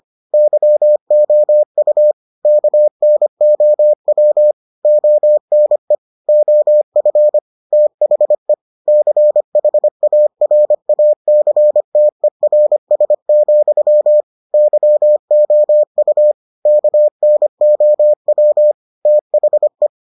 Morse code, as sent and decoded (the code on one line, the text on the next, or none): -.-- --- ..- / -.- -. --- .-- / --- -. / --- ..-. / - .... / -.-. .... .- .-. .- -.-. - . .-. ... --..-- / -.-- --- ..- / -.- -. --- .-- / - .... .
YOU KNOW ONE OF THE CHARACTERS, YOU KNOW THE